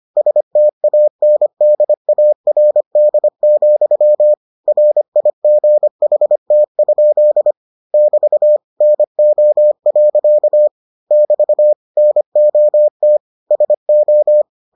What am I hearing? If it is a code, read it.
Morse code, heard as a keyed tone, about 25 wpm, 600 Hz. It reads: STANDARD, RIGHT? <BT> NO. <BT> NOT SO